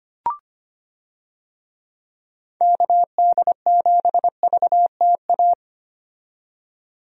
Morse code, heard as a keyed tone, about 25 wpm, 700 Hz. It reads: KD7VTA
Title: USA Call Signs - 6 Characters - 25wpm